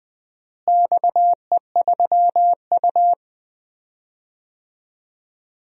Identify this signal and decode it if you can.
Morse code, heard as a keyed tone, about 20 wpm, 700 Hz. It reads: XE3U
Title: DXCC-100 Call Signs - 4 Characters - Repeated 2x - 20wpm